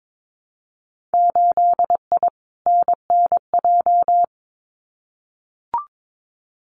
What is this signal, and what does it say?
Morse code, heard as a keyed tone, about 22 wpm, 700 Hz. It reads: 8I NNJ
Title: Winter Field Day Exchanges Encore 22wpm